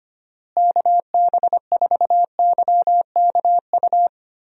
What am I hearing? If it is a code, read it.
Morse code, heard as a keyed tone, about 25 wpm, 700 Hz. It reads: KB4YKU
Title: USA Call Signs - 6 Characters - Repeated 2x - 25wpm